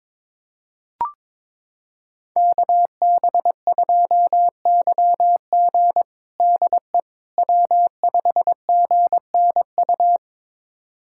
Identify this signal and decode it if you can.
Morse code, heard as a keyed tone, about 22 wpm, 700 Hz. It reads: KB2YG DE W5GNU
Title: USA Callsign DE Callsign - 5 Characters - 22wpm